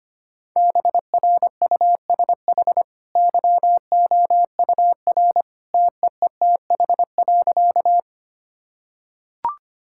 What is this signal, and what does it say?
Morse code, heard as a keyed tone, about 25 wpm, 700 Hz. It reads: BRUSH YOUR TEETH.
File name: Easy Sentences with Hints - 25wpm